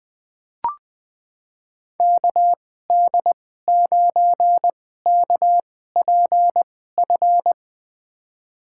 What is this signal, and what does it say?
Morse code, heard as a keyed tone, about 21 wpm, spaced as for 15 wpm, 700 Hz. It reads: KD9KPF
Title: USA Call Signs - 6 Characters - 2x Character Spacing - 20wpm